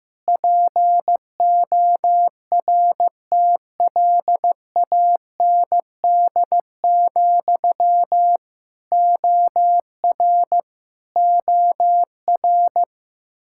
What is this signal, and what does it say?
Morse code, heard as a keyed tone, about 16 wpm, 700 Hz. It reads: PORTLAND, OR OR
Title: Practice QSO - Exchange 1 - Beginning - 15wpm